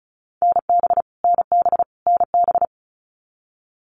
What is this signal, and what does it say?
Morse code, heard as a keyed tone, about 34 wpm, 700 Hz. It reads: NB NB NB
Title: Canadian Provinces and Territories - Repeated 3x - 35wpm